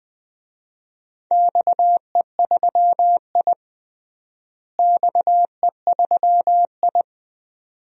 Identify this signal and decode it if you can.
Morse code, heard as a keyed tone, about 20 wpm, 700 Hz. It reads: XE3I XE3I
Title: DXCC-100 Call Signs - 4 Characters - Repeated 2x - 20wpm